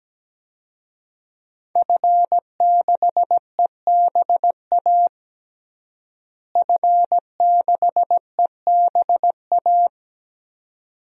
Morse code, heard as a keyed tone, about 17 wpm, 700 Hz. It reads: F6EBA F6EBA
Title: DXCC-100 Call Signs - 5 Characters - Repeated 3x - 17wpm